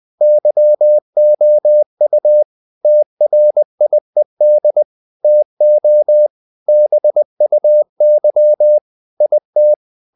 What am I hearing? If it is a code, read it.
Morse code, heard as a keyed tone, about 20 wpm, 600 Hz. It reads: YOU TRIED TO BUY IT
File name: the old man and the sea part 2 20wpm